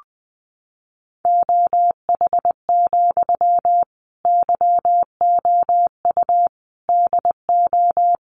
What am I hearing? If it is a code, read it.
Morse code, heard as a keyed tone, about 20 wpm, 700 Hz. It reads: OH, YOU DO
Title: Sentences from Top 600 Words 20wpm